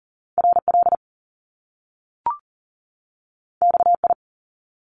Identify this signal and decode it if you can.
Morse code, heard as a keyed tone, about 40 wpm, 700 Hz. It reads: RL XI
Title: Top Two Letter Combinations in Words 40wpm